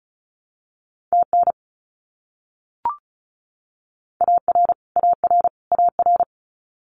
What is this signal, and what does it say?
Morse code, heard as a keyed tone, about 34 wpm, 700 Hz. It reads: TN AR AR AR
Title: US States Repeated 3x 35wpm